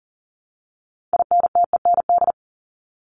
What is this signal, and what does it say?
Morse code, heard as a keyed tone, about 40 wpm, 700 Hz. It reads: INTEND
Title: Top 1501-1750 Words 40wpm